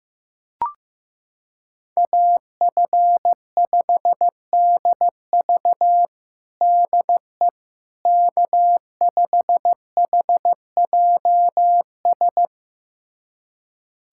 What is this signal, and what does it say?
Morse code, heard as a keyed tone, about 15 wpm, 700 Hz. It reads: AF5DV DE K5HJS
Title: USA Callsign DE Callsign - 5 Characters - 15wpm